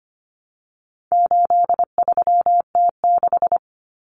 Morse code, heard as a keyed tone, about 25 wpm, 700 Hz. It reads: 83T6
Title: Common Cut Numbers - 4-Digit 25wpm